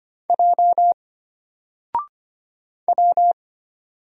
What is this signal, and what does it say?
Morse code, heard as a keyed tone, about 25 wpm, 700 Hz. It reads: J W
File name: Single-Letter-Number-25wpm